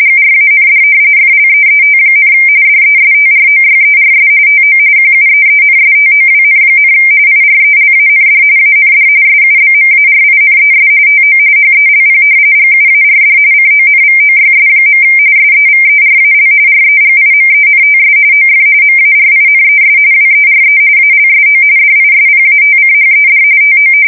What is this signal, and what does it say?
RTTY, about 45.45 baud. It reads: GRAND FLEET, DESTROYERS TO QUEENSTOWN, IRELAND AND SUBMARINES TO HELP GUARD CONVOYS. SEVERAL REGIMENTS OF MARINES WERE ALSO DISPATCHED TO
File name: fsk170 / usnavy4